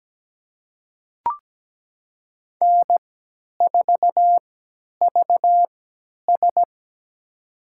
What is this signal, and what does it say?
Morse code, heard as a keyed tone, about 17 wpm, spaced as for 10 wpm, 700 Hz. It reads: N4VS
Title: USA Call Signs - 4 Characters - 3x Character Spacing - 17wpm